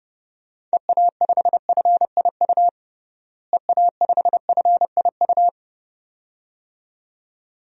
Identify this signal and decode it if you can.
Morse code, heard as a keyed tone, about 30 wpm, 700 Hz. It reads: EA5FIU EA5FIU
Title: DXCC-100 Call Signs - 6 Characters - Repeated 2x - 30wpm